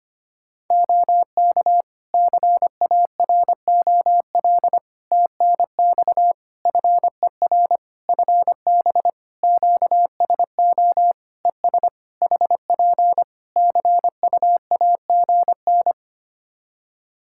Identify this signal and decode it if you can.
Morse code, heard as a keyed tone, about 25 wpm, 700 Hz. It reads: OK CAROL TNX FER FB QSO ES HP CUAGN
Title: Practice QSO - Exchange 4 - Ending - 25wpm